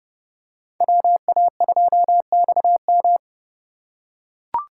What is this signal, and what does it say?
Morse code, heard as a keyed tone, about 29 wpm, 700 Hz. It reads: WA2XM